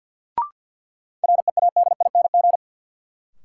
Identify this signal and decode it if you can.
Morse code, heard as a keyed tone, about 51 wpm, 700 Hz. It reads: READING
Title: Sentences from Top 600 Words - New Word Drill - 50wpm